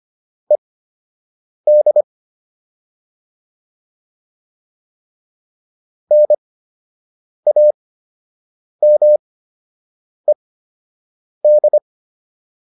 Morse code, heard as a keyed tone, about 25 wpm, spaced as for 7 wpm, 600 Hz. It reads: ED NAMED